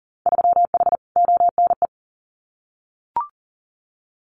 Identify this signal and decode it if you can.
Morse code, heard as a keyed tone, about 42 wpm, 700 Hz. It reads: 3H ONE
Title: Winter Field Day Exchanges Encore 40wpm